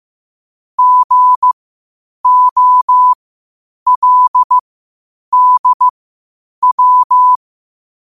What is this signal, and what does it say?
Morse code, heard as a keyed tone, about 15 wpm, spaced as for 8 wpm, 1000 Hz. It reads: GOLDW
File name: qotdfiles / 8wpm